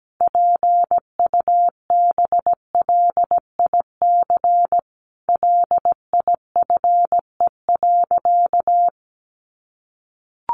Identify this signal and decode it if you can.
Morse code, heard as a keyed tone, about 17 wpm, 700 Hz. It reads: PUBLIC LIFE.
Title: Sentences from Top 600 Words 17wpm